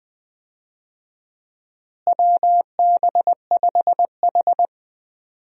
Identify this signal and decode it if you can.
Morse code, heard as a keyed tone, about 20 wpm, 700 Hz. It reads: WB5H
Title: USA Call Signs - 4 Characters - Repeated 3x - 20wpm